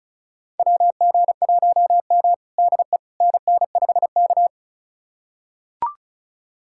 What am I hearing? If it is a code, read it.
Morse code, heard as a keyed tone, about 36 wpm, 690 Hz. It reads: WG1M DE NN5K
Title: USA Callsign DE Callsign - 4 Characters - 35wpm